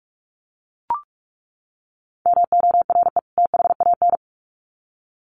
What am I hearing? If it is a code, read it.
Morse code, heard as a keyed tone, about 44 wpm, 700 Hz. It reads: MORE THAN
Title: Sets of 2 Words 45wpm